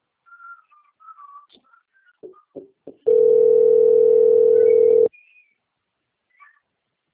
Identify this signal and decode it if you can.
Ringback tone (North America), 440 + 480 Hz.
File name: rg-1600-unknown- - - .29260